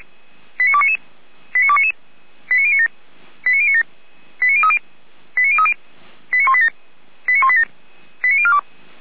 Five-tone selective call (ZVEI), heard as quick runs of five tones, tones of about 70 ms. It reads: 89299 89299 89097 89097 89030 89030 89187 89187 89042